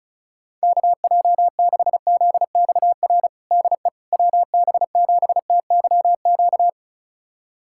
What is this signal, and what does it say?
Morse code, heard as a keyed tone, about 34 wpm, 690 Hz. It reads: KJ6ZXR DE WB7TYQ
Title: USA Callsign DE Callsign - 6 Characters - 35wpm